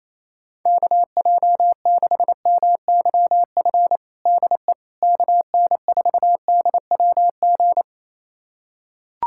Morse code, heard as a keyed tone, about 28 wpm, 700 Hz. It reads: KJ6MYF DE KN4DWG
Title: USA Callsign DE Callsign 28wpm